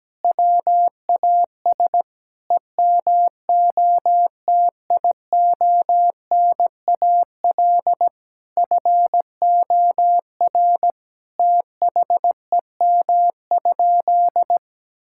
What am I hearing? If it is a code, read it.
Morse code, heard as a keyed tone, about 17 wpm, 700 Hz. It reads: WAS EMOTIONAL FOR THEM?
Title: Sentences from Top 1000 Words - Review Part 5 - 17wpm